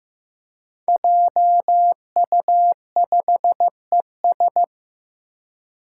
Morse code, heard as a keyed tone, about 15 wpm, 700 Hz. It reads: JU5ES
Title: Five Challenging Letter-Number Combinations 15wpm